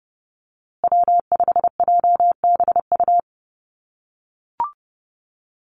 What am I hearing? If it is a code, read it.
Morse code, heard as a keyed tone, about 30 wpm, 700 Hz. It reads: W5JBU
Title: USA Call Signs Repeated 3x 30wpm